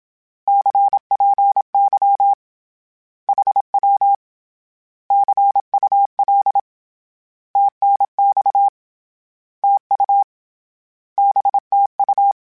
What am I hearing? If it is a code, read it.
Morse code, heard as a keyed tone, about 27 wpm, 800 Hz. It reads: CPY HW CUL TNX TU BTU
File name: Misc / Misc-mp3s / Words @25